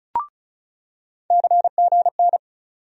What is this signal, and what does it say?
Morse code, heard as a keyed tone, about 34 wpm, 700 Hz. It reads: CGN
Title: USA Call Sign Postfixes - 3 Letters 35wpm